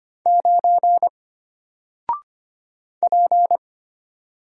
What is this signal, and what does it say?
Morse code, heard as a keyed tone, about 25 wpm, 700 Hz. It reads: 9 P
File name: Single-Letter-Number-25wpm